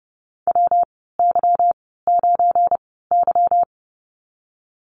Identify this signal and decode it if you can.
Morse code, heard as a keyed tone, about 30 wpm, spaced as for 17 wpm, 700 Hz. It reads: WY9Y
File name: Lesson 37g - Letter x - Call Signs 30wpm